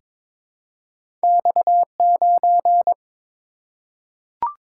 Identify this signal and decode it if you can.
Morse code, heard as a keyed tone, about 22 wpm, 700 Hz. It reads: X9
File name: practice / 22wpm / Two Letter-Number Combinations - 22wpm